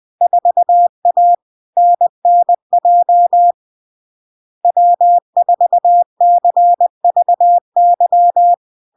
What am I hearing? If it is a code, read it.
Morse code, heard as a keyed tone, about 20 wpm, 700 Hz. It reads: 4A NNJ W4CVY
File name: LARK-FD-2022-Log-in-CW-20wpm-700-Hz